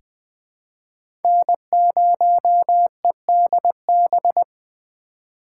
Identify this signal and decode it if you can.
Morse code, heard as a keyed tone, about 21 wpm, 700 Hz. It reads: N0EDB
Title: USA Call Signs - 2x Character Spacing - 20wpm